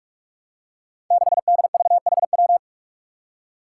Morse code, heard as a keyed tone, about 46 wpm, 700 Hz. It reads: 6DVHW
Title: Five Challenging Letter-Number Combinations 45wpm